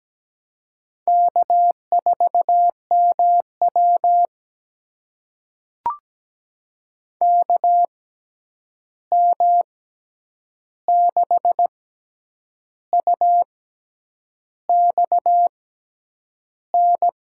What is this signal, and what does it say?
Morse code, heard as a keyed tone, about 17 wpm, spaced as for 8 wpm, 700 Hz. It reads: K4MW KM6UXN
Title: USA Call Signs - 6x Character Spacing - 17wpm